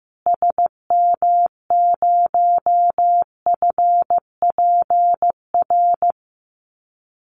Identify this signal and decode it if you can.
Morse code, heard as a keyed tone, about 15 wpm, 700 Hz. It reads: SM0FPR
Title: Call Signs Repeated 2x - 15wpm